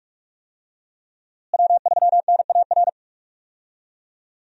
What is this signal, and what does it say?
Morse code, heard as a keyed tone, about 44 wpm, 700 Hz. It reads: W3NAR
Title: USA Call Signs - 5 Characters - Repeated 3x - 45wpm